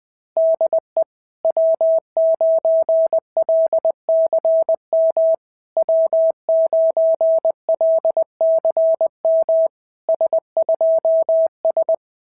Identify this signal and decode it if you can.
Morse code, heard as a keyed tone, about 20 wpm, 660 Hz. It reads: DE W9LCM W9LCM S2S
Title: SOTA QSO Practice 20wpm